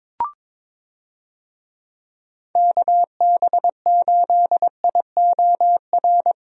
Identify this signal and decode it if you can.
Morse code, heard as a keyed tone, about 22 wpm, 700 Hz. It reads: KB8IOR
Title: USA Call Signs - 6 Characters - Repeated 3x - 22wpm